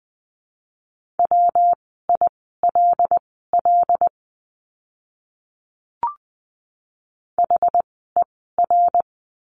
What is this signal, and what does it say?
Morse code, heard as a keyed tone, about 20 wpm, spaced as for 14 wpm, 700 Hz. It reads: WILL HER
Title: 244 Lesson 39f - Letter z - Words 20wpm